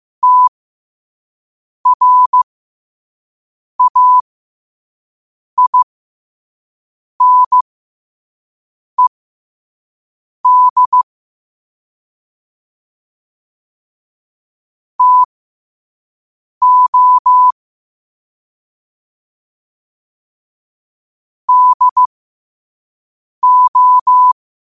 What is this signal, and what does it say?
Morse code, heard as a keyed tone, about 15 wpm, spaced as for 5 wpm, 1000 Hz. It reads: TRAINED TO DO
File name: qotdfiles / 5wpm